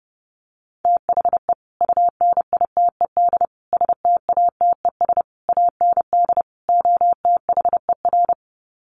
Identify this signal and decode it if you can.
Morse code, heard as a keyed tone, about 30 wpm, 700 Hz. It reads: THE UNITED STATES AND OTHER
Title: Sets of 5 Words Encore 30wpm